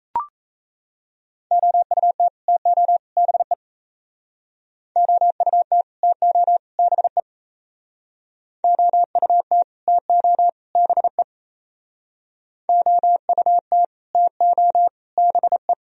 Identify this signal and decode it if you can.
Morse code, heard as a keyed tone, about 38 wpm, 700 Hz. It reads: OUT TO BE OUT TO BE OUT TO BE OUT TO BE